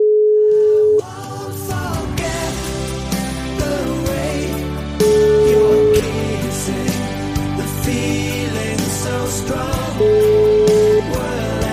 Ringback tone (Europe, 425 Hz), with noise.